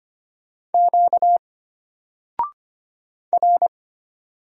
Morse code, heard as a keyed tone, about 25 wpm, 700 Hz. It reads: Q R
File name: Single-Letter-Number-25wpm